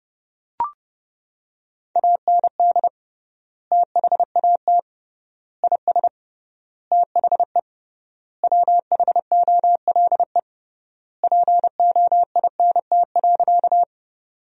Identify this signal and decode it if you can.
Morse code, heard as a keyed tone, about 30 wpm, 700 Hz. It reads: AND THAT IS THE WHOLE POINT.
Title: 3x Word-Spacing - Sentences from Top 400 Words - 30wpm